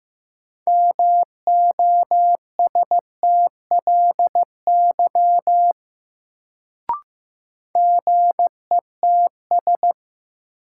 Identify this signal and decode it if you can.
Morse code, heard as a keyed tone, about 15 wpm, 700 Hz. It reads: MOSTLY GETS
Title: Sentences from Top 1300 Words - New Word Drill Alt - 15wpm